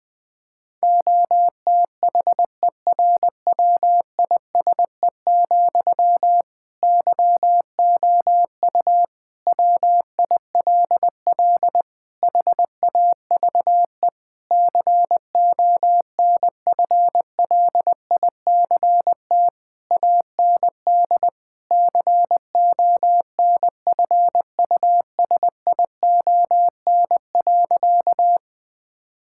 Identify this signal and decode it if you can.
Morse code, heard as a keyed tone, about 21 wpm, 700 Hz. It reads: OTHERWISE, YOU WILL HAVE CONFLICT AND CONFUSION.
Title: Sentences from Top 1600 Words 20wpm